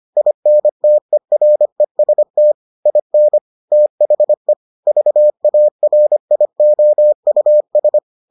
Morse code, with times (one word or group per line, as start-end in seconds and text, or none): 0.2-2.5 INTEREST
2.9-3.4 IN
3.7-4.5 THE
4.9-8.0 VARIOUS